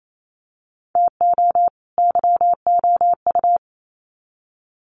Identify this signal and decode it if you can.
Morse code, heard as a keyed tone, about 28 wpm, 700 Hz. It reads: TO YOU
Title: Sets of 2 Words 28wpm